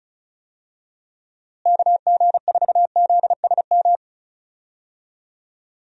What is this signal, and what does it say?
Morse code, heard as a keyed tone, about 36 wpm, 700 Hz. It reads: KG4ZSM
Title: USA Call Signs - 6 Characters - Repeated 2x - 35wpm